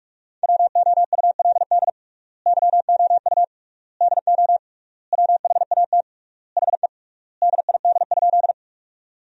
Morse code, heard as a keyed tone, about 46 wpm, 700 Hz. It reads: WOULD YOU DO WHAT HE DID?
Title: 3x Word-Spacing - Sentences from Top 100 Words - 45wpm